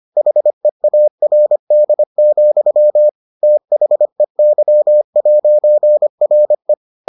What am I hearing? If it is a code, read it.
Morse code, heard as a keyed tone, about 25 wpm, 600 Hz. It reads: HEARD, THEY'RE